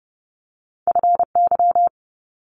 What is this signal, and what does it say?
Morse code, heard as a keyed tone, about 30 wpm, 700 Hz. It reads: FY
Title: Multi-letter Phonemes Quick Fire 30wpm